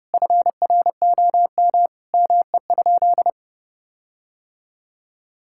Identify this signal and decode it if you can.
Morse code, heard as a keyed tone, about 31 wpm, 700 Hz. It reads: FROM ME?
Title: Sentences from Top 1600 Words 30wpm